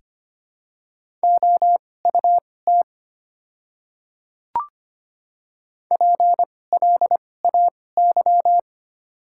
Morse code, heard as a keyed tone, about 25 wpm, spaced as for 18 wpm, 700 Hz. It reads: OUT PLAY
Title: 2x Character Spacing - Top 200 Words - 25wpm